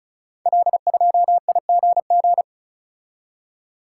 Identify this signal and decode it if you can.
Morse code, heard as a keyed tone, about 36 wpm, 700 Hz. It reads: L2IGG
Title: Code Groups - 5 Letter:Numbers - 35wpm